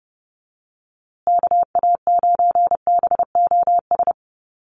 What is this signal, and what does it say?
Morse code, heard as a keyed tone, about 30 wpm, 700 Hz. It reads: KA9BOS
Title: USA Call Signs Repeated 3x 30wpm